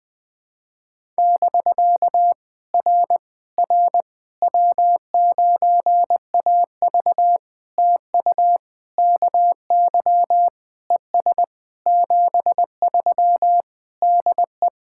Morse code, heard as a keyed tone, about 20 wpm, 700 Hz. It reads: <BK> R R W9AV TU KY ES 73 DE
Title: POTA QSO Practice - 20wpm